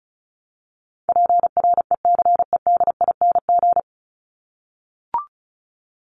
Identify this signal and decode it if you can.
Morse code, heard as a keyed tone, about 34 wpm, 700 Hz. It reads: PRECEDING
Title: ING Suffix - Top 100 Words with Base Words 35wpm